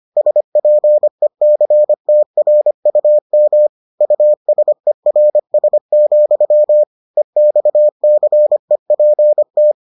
Morse code, heard as a keyed tone, about 25 wpm, 600 Hz. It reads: SPECTRUM USERS, EXCEPT